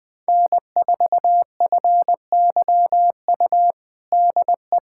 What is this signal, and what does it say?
Morse code, heard as a keyed tone, about 21 wpm, 700 Hz. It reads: N4FYU DE
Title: USA Callsign DE Callsign - 5 Characters - 20wpm